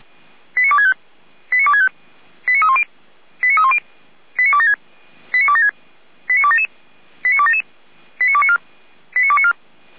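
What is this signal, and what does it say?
Five-tone selective call (ZVEI), heard as quick runs of five tones, tones of about 70 ms. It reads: 89265 89265 89310 89310 89276 89276 89277 89277 89294 89294
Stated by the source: ZVEI five-tone selective call 89173 received by SDR, from the fire brigade Feuerwehr Orlamünde audio archive.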